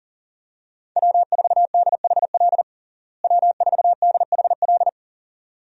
Morse code, heard as a keyed tone, about 40 wpm, 700 Hz. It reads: W4DHL W4DHL
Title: USA Call Signs - 5 Characters - Repeated 2x - 40wpm